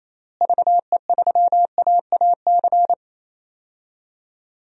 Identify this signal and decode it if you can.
Morse code, heard as a keyed tone, about 28 wpm, 700 Hz. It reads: VE3AAC
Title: DXCC-100 Call Signs - 6 Characters - Repeated 3x - 28wpm